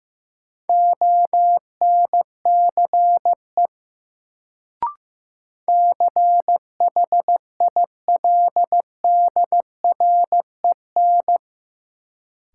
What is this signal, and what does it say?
Morse code, heard as a keyed tone, about 15 wpm, 700 Hz. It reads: ONCE CHILDREN
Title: Sentences from Top 300 Words - New Word Drill - 15wpm